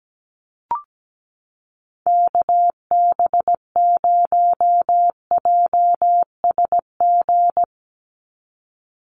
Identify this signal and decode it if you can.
Morse code, heard as a keyed tone, about 17 wpm, 700 Hz. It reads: KB0JSG
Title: USA Call Signs 17wpm